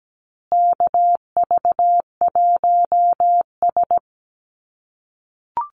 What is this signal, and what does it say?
Morse code, heard as a keyed tone, about 17 wpm, 700 Hz. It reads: KV1S
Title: USA Call Signs Repeated 2x 17wpm